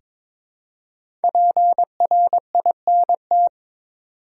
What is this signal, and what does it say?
Morse code, heard as a keyed tone, about 22 wpm, 700 Hz. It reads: PRINT
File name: 5 Letter Words - First 3 Letters Given - 22wpm